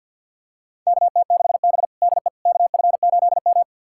Morse code, heard as a keyed tone, about 52 wpm, 700 Hz. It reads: KT6B DE KF8M
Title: USA Callsign DE Callsign - 4 Characters - 50wpm